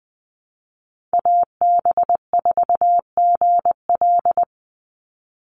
Morse code, heard as a keyed tone, about 20 wpm, 700 Hz. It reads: AB4GL
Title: USA Call Signs 20wpm